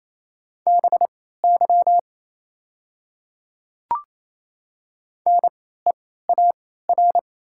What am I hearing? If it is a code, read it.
Morse code, heard as a keyed tone, about 28 wpm, spaced as for 16 wpm, 690 Hz. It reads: BY NEAR
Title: 3x Character Spacing - Top 300 Words - 28wpm